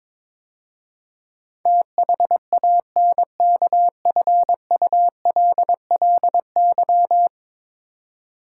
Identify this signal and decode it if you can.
Morse code, heard as a keyed tone, about 22 wpm, 700 Hz. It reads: THANKFULLY
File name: practice / 22wpm / 10 Letter Words - First 3 Letters Given - 22wpm